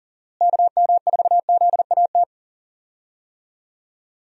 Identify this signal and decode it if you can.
Morse code, heard as a keyed tone, about 40 wpm, 700 Hz. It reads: KM4ZAT
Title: USA Call Signs - 6 Characters - Repeated 3x - 40wpm